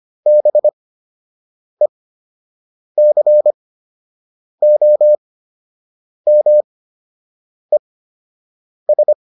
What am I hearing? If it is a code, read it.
Morse code, heard as a keyed tone, about 25 wpm, spaced as for 7 wpm, 600 Hz. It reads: BECOMES